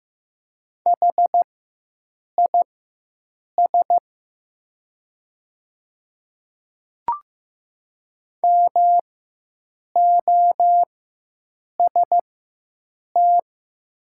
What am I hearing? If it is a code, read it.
Morse code, heard as a keyed tone, about 15 wpm, spaced as for 7 wpm, 700 Hz. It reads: HIS MOST
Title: 4x Character Spacing - Top 200 Words - 15wpm